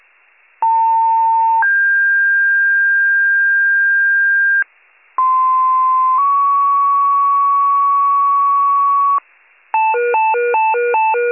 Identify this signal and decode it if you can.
Two-tone page, heard as a one-second tone, then a three-second tone, about 900 then 1600 Hz, about 1040 then 1120 Hz. It is heard from 0.6 to 9.2 s.